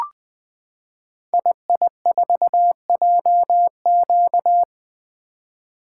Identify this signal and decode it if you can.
Morse code, heard as a keyed tone, about 20 wpm, 700 Hz. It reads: II4JQ
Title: Code Groups - 5 Letter:Numbers - 20wpm